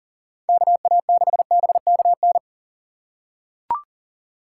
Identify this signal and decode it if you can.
Morse code, heard as a keyed tone, about 37 wpm, 700 Hz. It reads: KA6BKN
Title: USA Call Signs - 6 Characters - 6x Character Spacing - 40wpm